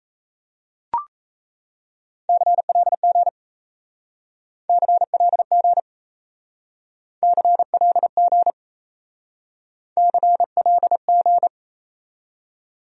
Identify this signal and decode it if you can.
Morse code, heard as a keyed tone, about 35 wpm, 700 Hz. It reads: CLG CLG CLG CLG